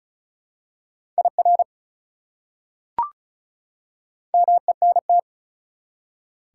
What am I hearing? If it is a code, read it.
Morse code, heard as a keyed tone, about 36 wpm, 690 Hz. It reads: IR MENT
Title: Prefixes and Suffixes 35wpm